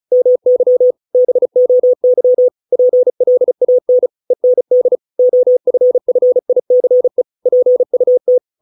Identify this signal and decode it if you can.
Morse code, heard as a keyed tone, about 34 wpm, 500 Hz. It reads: MY BOY PLAN END OFFICE PUT